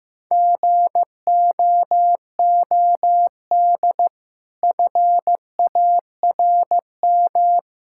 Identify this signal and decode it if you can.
Morse code, heard as a keyed tone, about 15 wpm, 700 Hz. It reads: GOOD FARM
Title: Sentences from Top 1000 Words - Review Part 5 - 15wpm